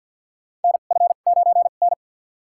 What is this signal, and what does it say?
Morse code, heard as a keyed tone, about 51 wpm, spaced as for 36 wpm, 700 Hz. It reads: NF9N
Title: USA Call Signs - 2x Character Spacing - 50wpm